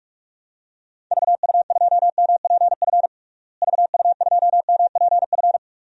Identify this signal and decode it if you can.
Morse code, heard as a keyed tone, about 46 wpm, 700 Hz. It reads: VU2MPF VU2MPF
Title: DXCC-100 Call Signs - 6 Characters - Repeated 2x - 45wpm